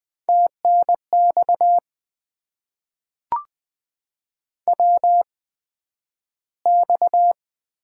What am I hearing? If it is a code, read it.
Morse code, heard as a keyed tone, about 20 wpm, spaced as for 11 wpm, 700 Hz. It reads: TNX WX
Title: Top 90 QSO Elements - 8x Character Spacing - 20wpm